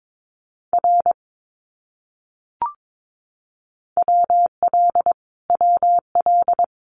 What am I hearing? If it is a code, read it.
Morse code, heard as a keyed tone, about 22 wpm, 700 Hz. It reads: R WL WL
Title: Abbreviations Repeated 3x - 22wpm